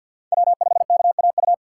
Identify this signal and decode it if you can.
Morse code, heard as a keyed tone, about 53 wpm, 700 Hz. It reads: W5KAU
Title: USA Call Signs - 5 Characters - 3x Character Spacing - 50wpm